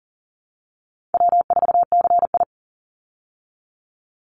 Morse code, heard as a keyed tone, about 40 wpm, 700 Hz. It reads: W4CI
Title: Call Signs Repeated 3x - 40wpm